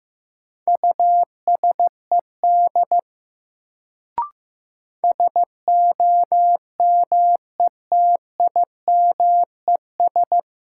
Morse code, heard as a keyed tone, about 15 wpm, 700 Hz. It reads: USED SOMETIMES
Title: Sentences from Top 600 Words - New Word Drill - 15wpm